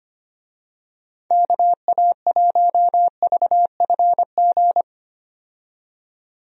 Morse code, heard as a keyed tone, about 25 wpm, 700 Hz. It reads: KA1VFG
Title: USA Call Signs - 6 Characters - 25wpm